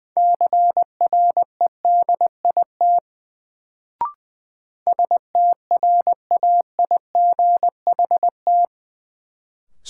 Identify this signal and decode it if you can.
Morse code, heard as a keyed tone, about 20 wpm, 700 Hz. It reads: CREDIT STRAIGHT
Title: Sentences from Top 1000 Words - New Word Drill - 20wpm